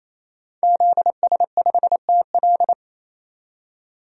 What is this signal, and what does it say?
Morse code, heard as a keyed tone, about 27 wpm, 700 Hz. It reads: ZS5TL
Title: DXCC-100 Call Signs - 5 Characters - 28wpm